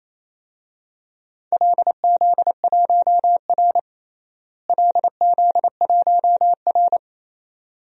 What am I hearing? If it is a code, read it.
Morse code, heard as a keyed tone, about 28 wpm, 700 Hz. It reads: LZ1R LZ1R
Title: DXCC-100 Call Signs - 4 Characters - Repeated 2x - 28wpm